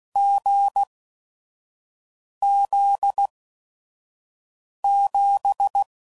Morse code, heard as a keyed tone, about 16 wpm, spaced as for 5 wpm, 800 Hz. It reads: GZ7